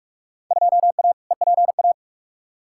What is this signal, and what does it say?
Morse code, heard as a keyed tone, about 43 wpm, 700 Hz. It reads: 2A EPA
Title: Field Day Exchanges - 45wpm